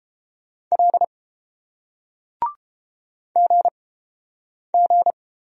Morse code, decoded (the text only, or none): L GG